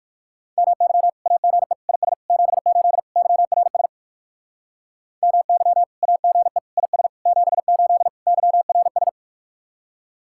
Morse code, heard as a keyed tone, about 69 wpm, spaced as for 58 wpm, 700 Hz. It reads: MY AGE IS 78 YRS MY AGE IS 78 YRS